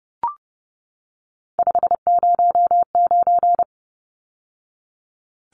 Morse code, heard as a keyed tone, about 30 wpm, 700 Hz. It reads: 509
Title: Numbers - 3-Digit 30wpm